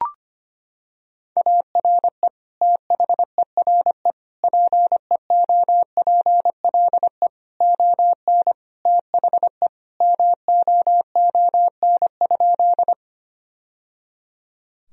Morse code, heard as a keyed tone, about 25 wpm, 700 Hz. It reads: ARE THERE PEOPLE ON THE MOON?
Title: Sentences from Top 1000 Words - Review Part 3 - 25wpm